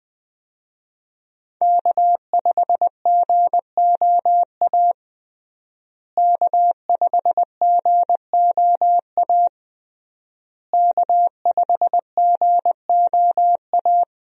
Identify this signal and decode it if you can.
Morse code, heard as a keyed tone, about 21 wpm, 700 Hz. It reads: K5GOA K5GOA K5GOA
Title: USA Call Signs - 5 Characters - Repeated 3x - 20wpm